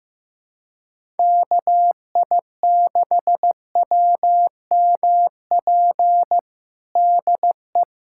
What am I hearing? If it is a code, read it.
Morse code, heard as a keyed tone, about 16 wpm, 700 Hz. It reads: KI6WMP DE
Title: USA Callsign DE Callsign 15wpm